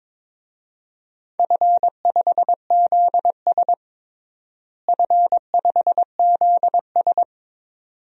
Morse code, heard as a keyed tone, about 22 wpm, 700 Hz. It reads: F5ZS F5ZS
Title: DXCC-100 Call Signs - 4 Characters - Repeated 2x - 22wpm